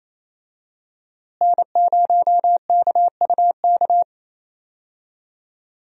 Morse code, heard as a keyed tone, about 28 wpm, 700 Hz. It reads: N0KUK
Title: USA Call Signs - 5 Characters - Repeated 2x - 28wpm